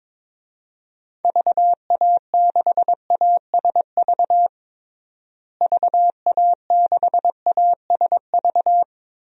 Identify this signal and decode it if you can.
Morse code, heard as a keyed tone, about 22 wpm, 700 Hz. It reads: VA6ASV VA6ASV
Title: DXCC-100 Call Signs - 6 Characters - Repeated 2x - 22wpm